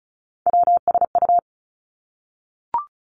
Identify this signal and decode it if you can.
Morse code, heard as a keyed tone, about 36 wpm, 700 Hz. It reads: WSU
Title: Three Challenging Letter-Number Combinations 35wpm